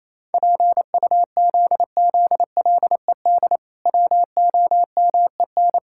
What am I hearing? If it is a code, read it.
Morse code, heard as a keyed tone, about 28 wpm, 700 Hz. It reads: PUZZLED WOMEN
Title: Pangram Challenge v2 - 28wpm